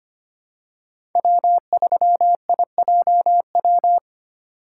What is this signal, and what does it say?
Morse code, heard as a keyed tone, about 25 wpm, 700 Hz. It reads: W3IJW
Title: USA Call Signs - 5 Characters - 6x Character Spacing - 25wpm